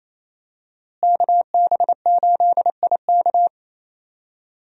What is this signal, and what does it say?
Morse code, heard as a keyed tone, about 27 wpm, 700 Hz. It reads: KB8IK
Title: USA Call Signs - 5 Characters - Repeated 3x - 28wpm